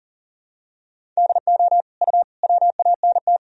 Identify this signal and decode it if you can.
Morse code, heard as a keyed tone, about 40 wpm, 700 Hz. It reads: DO U WANT
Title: Sentences from Top 300 Words and Abbreviations 40wpm